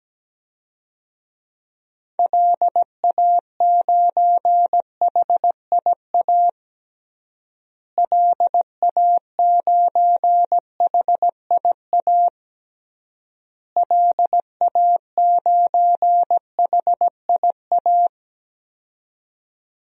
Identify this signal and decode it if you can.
Morse code, heard as a keyed tone, about 17 wpm, 700 Hz. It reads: LA9HIA LA9HIA LA9HIA
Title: DXCC-100 Call Signs - 6 Characters - Repeated 3x - 17wpm